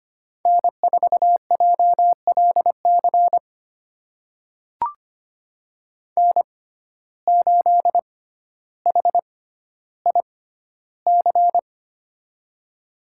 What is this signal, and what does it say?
Morse code, heard as a keyed tone, about 25 wpm, spaced as for 13 wpm, 700 Hz. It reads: N4JLC N8HIC
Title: USA Call Signs - 6x Character Spacing - 25wpm